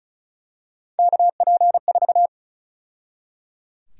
Morse code, heard as a keyed tone, about 34 wpm, 700 Hz. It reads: KP4
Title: USA Call Sign Prefixes - 35wpm